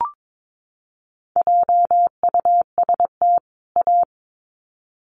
Morse code, heard as a keyed tone, about 22 wpm, 700 Hz. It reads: JUST A
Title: Sets of 2 Words 22wpm